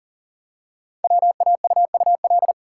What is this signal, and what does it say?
Morse code, heard as a keyed tone, about 40 wpm, 700 Hz. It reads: WAUUL
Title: Code Groups - 5 Letters - 40wpm